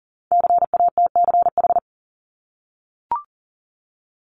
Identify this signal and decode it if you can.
Morse code, heard as a keyed tone, about 42 wpm, 690 Hz. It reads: CATCH
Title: Multi-letter Phonemes 40wpm